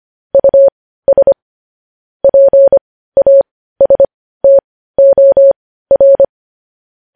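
Morse code, heard as a keyed tone, about 25 wpm, spaced as for 15 wpm, 550 Hz. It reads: US PASTOR
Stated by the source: News headlines sent in Morse code at 15 wpm.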